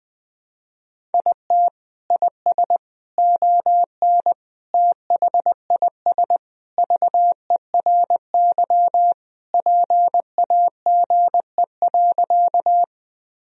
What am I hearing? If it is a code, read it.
Morse code, heard as a keyed tone, about 20 wpm, 700 Hz. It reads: IT IS ON THIS VERY PAGE.
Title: Sentences from Top 1000 Words - Review Part 1 - 20wpm